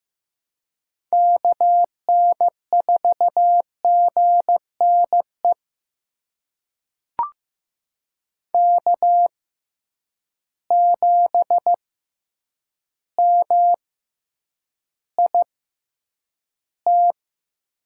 Morse code, heard as a keyed tone, about 15 wpm, spaced as for 8 wpm, 700 Hz. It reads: KN4GNE K7MIT
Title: USA Call Signs - 6x Character Spacing - 15wpm